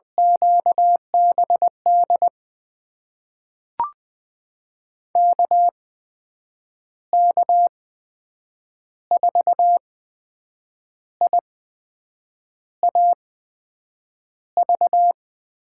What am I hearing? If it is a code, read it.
Morse code, heard as a keyed tone, about 20 wpm, spaced as for 7 wpm, 700 Hz. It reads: QBD KK4IAV
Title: USA Call Signs - 6 Characters - 8x Character Spacing - 20wpm